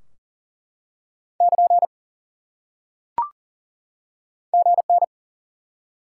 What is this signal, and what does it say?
Morse code, heard as a keyed tone, about 40 wpm, 700 Hz. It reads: <KN> GN
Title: Top 90 QSO Elements 40wpm